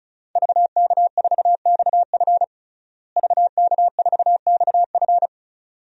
Morse code, heard as a keyed tone, about 36 wpm, 700 Hz. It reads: VK4XF VK4XF
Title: DXCC-100 Call Signs - 5 Characters - Repeated 2x - 35wpm